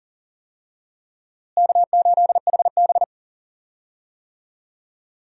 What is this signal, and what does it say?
Morse code, heard as a keyed tone, about 40 wpm, 700 Hz. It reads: K8HB
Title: USA Call Signs - 4 Characters - Repeated 2x - 40wpm